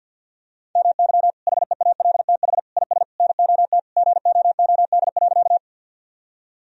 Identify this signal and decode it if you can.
Morse code, heard as a keyed tone, about 50 wpm, 700 Hz. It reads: MY HEALTH IS NOT GOOD.
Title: Sentences from Top 1000 Words - Review Part 3 - 50wpm